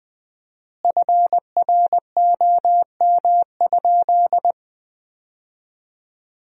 Morse code, heard as a keyed tone, about 20 wpm, 700 Hz. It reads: FROM?
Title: 4x Word-Spacing - Sentences from Top 200 Words - 20wpm